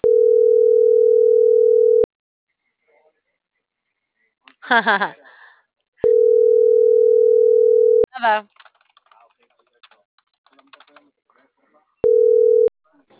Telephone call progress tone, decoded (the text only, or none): ringback tone